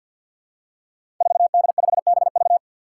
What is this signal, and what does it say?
Morse code, heard as a keyed tone, about 47 wpm, 700 Hz. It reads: 4D5BV
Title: Five Challenging Letter-Number Combinations 50wpm